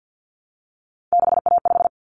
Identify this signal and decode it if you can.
Morse code, heard as a keyed tone, about 50 wpm, 700 Hz. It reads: 6A5
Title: Common Cut Numbers - 3-Digit 50wpm